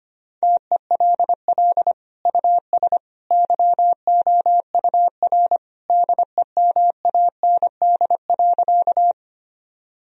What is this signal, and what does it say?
Morse code, heard as a keyed tone, about 25 wpm, 700 Hz. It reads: TELL US YOUR DEMAND.
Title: Sentences from Top 1000 Words - Review Part 4 - 25wpm